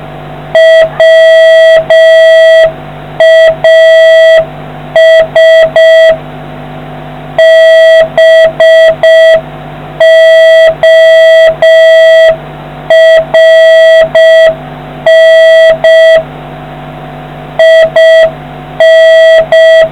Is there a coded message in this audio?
Yes: Morse code from 0.6 to 19.8 s, heard as a keyed tone, about 6 wpm, 650 Hz, with noise. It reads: WAS BORN IN